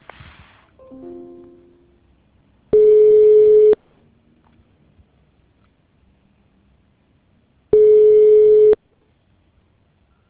Ringback tone (Europe, 425 Hz).